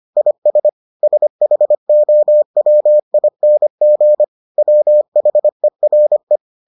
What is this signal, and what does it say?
Morse code, heard as a keyed tone, about 25 wpm, 600 Hz. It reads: IS SHOWING WHERE